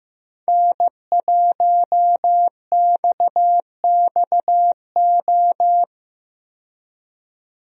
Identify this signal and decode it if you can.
Morse code, heard as a keyed tone, about 15 wpm, 700 Hz. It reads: N1XXO
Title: USA Call Signs - 5 Characters - 15wpm